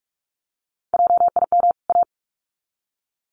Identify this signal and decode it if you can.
Morse code, heard as a keyed tone, about 44 wpm, 700 Hz. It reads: JIM A
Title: CWT Exchange 45wpm